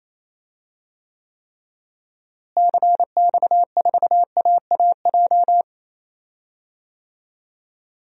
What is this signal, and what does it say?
Morse code, heard as a keyed tone, about 28 wpm, 700 Hz. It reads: CX4AAJ
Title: DXCC-100 Call Signs - 6 Characters - Repeated 3x - 28wpm